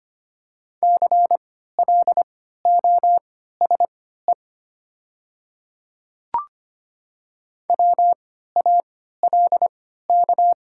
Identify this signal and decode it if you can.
Morse code, heard as a keyed tone, about 25 wpm, spaced as for 14 wpm, 700 Hz. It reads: CLOSE WALK
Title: 3x Character Spacing - Top 300 Words - 25wpm